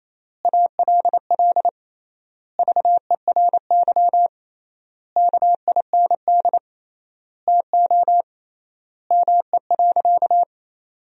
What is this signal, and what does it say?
Morse code, heard as a keyed tone, about 28 wpm, 700 Hz. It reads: ALL VERY KIND TO ME.
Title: 3x Word-Spacing - Sentences from Top 200 Words - 28wpm